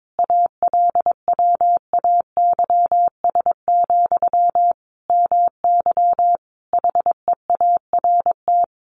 Morse code, heard as a keyed tone, about 22 wpm, 700 Hz. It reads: ALWAYS, MY HEART
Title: Sentences from Top 1200 Words 22wpm